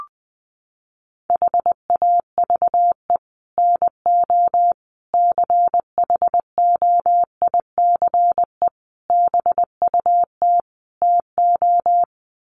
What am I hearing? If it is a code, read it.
Morse code, heard as a keyed tone, about 20 wpm, 700 Hz. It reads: HAVE NO CHOICE BUT TO
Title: Sets of 5 Words Encore 20wpm